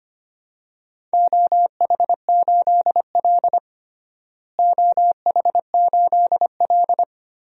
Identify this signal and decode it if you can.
Morse code, heard as a keyed tone, about 25 wpm, 700 Hz. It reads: OH8L OH8L
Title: DXCC-100 Call Signs - 4 Characters - Repeated 2x - 25wpm